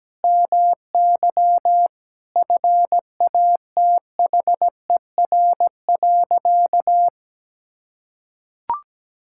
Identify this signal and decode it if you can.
Morse code, heard as a keyed tone, about 17 wpm, 700 Hz. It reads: MY FATHER.
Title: Sentences from Top 1000 Words - Review Part 5 - 17wpm